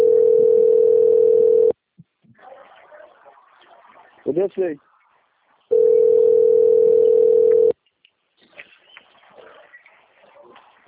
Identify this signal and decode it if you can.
Ringback tone (North America), 440 + 480 Hz, with speech.
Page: Dantas - PE